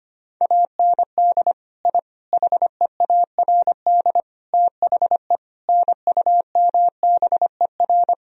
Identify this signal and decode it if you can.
Morse code, heard as a keyed tone, about 25 wpm, 700 Hz. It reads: AND I HEARD THE NUMBER
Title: Sentences from Top 1000 Words - Review Part 1 - 25wpm